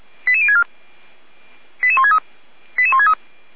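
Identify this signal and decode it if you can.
Five-tone selective call (ZVEI), heard as quick runs of five tones, tones of about 70 ms. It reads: 88064 88163 88163
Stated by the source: ZVEI five-tone selective call 88063, received by SDR.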